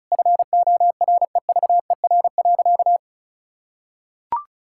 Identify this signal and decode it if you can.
Morse code, heard as a keyed tone, about 36 wpm, 690 Hz. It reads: FOREVER.
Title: Sentences from Top 1600 Words 35wpm